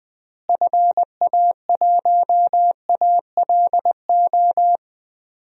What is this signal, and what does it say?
Morse code, heard as a keyed tone, about 20 wpm, 700 Hz. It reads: FA1ALO
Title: DXCC-100 Call Signs - 6 Characters - Repeated 2x - 20wpm